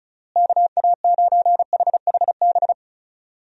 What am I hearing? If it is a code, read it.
Morse code, heard as a keyed tone, about 36 wpm, 700 Hz. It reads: KA9HHB
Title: USA Call Signs - 6 Characters - Repeated 3x - 35wpm